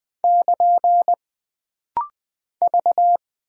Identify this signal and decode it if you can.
Morse code, heard as a keyed tone, about 20 wpm, 700 Hz. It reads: <KN> V
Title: Single Character (Letter:Number:Punctuation:Prosign) - Rapid-Fire - 20wpm